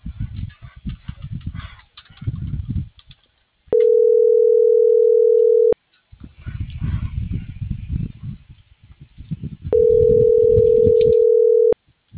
Ringback tone (North America), 440 + 480 Hz.